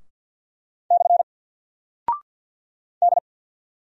Morse code, heard as a keyed tone, about 48 wpm, 700 Hz. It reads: / D
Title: Single Character (Letter:Number:Punctuation:Prosign) 50wpm